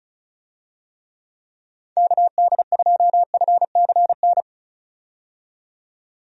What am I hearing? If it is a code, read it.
Morse code, heard as a keyed tone, about 36 wpm, 700 Hz. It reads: KD2FCN
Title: USA Call Signs - 6 Characters - 35wpm